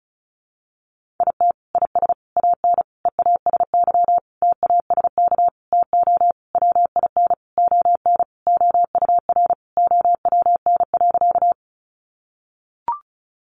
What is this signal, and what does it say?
Morse code, heard as a keyed tone, about 34 wpm, 700 Hz. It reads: IT IS AN EASY TASK TO WIN ON OUR OWN.
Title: Sentences from Top 1500 Words 35wpm